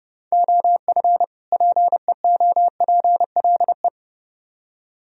Morse code, heard as a keyed tone, about 29 wpm, 700 Hz. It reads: OF PEOPLE